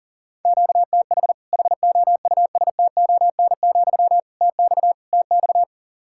Morse code, heard as a keyed tone, about 40 wpm, 700 Hz. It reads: QTH HOUSTON, TX TX
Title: Practice QSO - Exchange 1 - Beginning - 40wpm